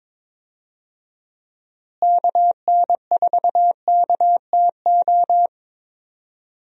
Morse code, heard as a keyed tone, about 21 wpm, 700 Hz. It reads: KN4KTO
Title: USA Call Signs - 6 Characters - Repeated 3x - 22wpm